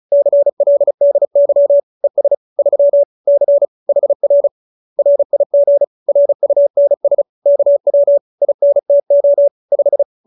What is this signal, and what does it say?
Morse code, heard as a keyed tone, about 34 wpm, 570 Hz. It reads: CLDY ES 3 C HR RIG RUNS KW INTO 5